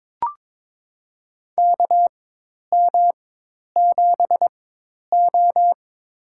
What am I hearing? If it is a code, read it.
Morse code, heard as a keyed tone, about 22 wpm, spaced as for 10 wpm, 700 Hz. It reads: KM7O
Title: USA Call Signs - 4 Characters - 4x Character Spacing - 22wpm